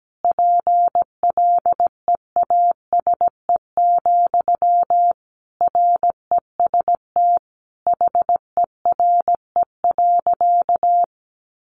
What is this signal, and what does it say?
Morse code, heard as a keyed tone, about 17 wpm, 700 Hz. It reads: PLEASE, REST HERE.
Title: Sentences from Top 1500 Words 17wpm